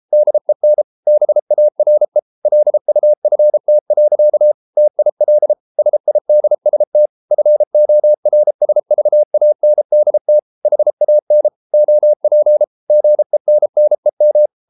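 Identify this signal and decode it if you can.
Morse code, heard as a keyed tone, about 33 wpm, 600 Hz. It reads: DEN BARE LUFT. TIL SIDST FORSVANDT HAN OP GENNEM
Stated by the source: Morse code at 33 wpm, QRQ.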